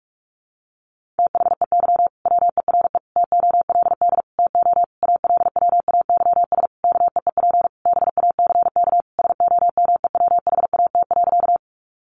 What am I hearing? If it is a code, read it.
Morse code, heard as a keyed tone, about 46 wpm, 700 Hz. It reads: THEY WERE TOLD TO ALWAYS KEEP BACK SOMEWHAT.
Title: Sentences from Top 1300 Words 45wpm